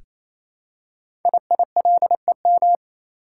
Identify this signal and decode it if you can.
Morse code, heard as a keyed tone, about 29 wpm, 700 Hz. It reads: IILEM
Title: Code Groups - 5 Letter:Numbers - 28wpm